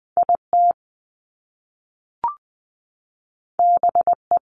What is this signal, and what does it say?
Morse code, heard as a keyed tone, about 20 wpm, 700 Hz. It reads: IT BE
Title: Top Two Letter Combinations in Words 20wpm